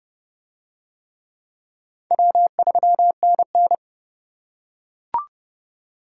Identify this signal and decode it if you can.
Morse code, heard as a keyed tone, about 26 wpm, 700 Hz. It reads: W3NN